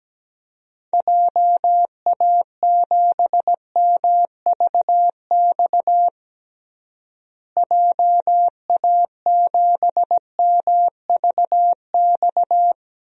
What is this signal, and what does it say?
Morse code, heard as a keyed tone, about 17 wpm, 700 Hz. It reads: JA7MVX JA7MVX
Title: DXCC-100 Call Signs - 6 Characters - Repeated 3x - 17wpm